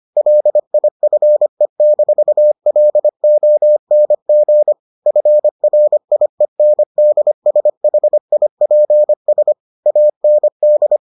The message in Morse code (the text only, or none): LIFE-LONG FRIENDSHIPS AND